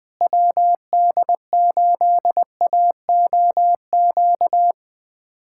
Morse code, heard as a keyed tone, about 19 wpm, 700 Hz. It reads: WD8AOQ
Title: USA Callsign DE Callsign - 6 Characters - 20wpm